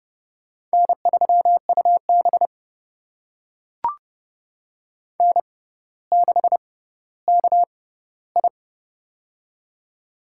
Morse code, heard as a keyed tone, about 30 wpm, spaced as for 15 wpm, 700 Hz. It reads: N3UB N6KI
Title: USA Call Signs - 4 Characters - 6x Character Spacing - 30wpm